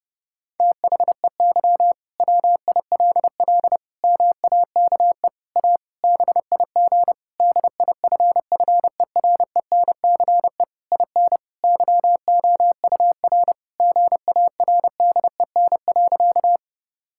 Morse code, heard as a keyed tone, about 32 wpm, 700 Hz. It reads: THEY WILL MAKE A BIG DIFFERENCE IN YOUR GARDEN.
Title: Sentences from Top 1000 Words - Review Part 5 - 30wpm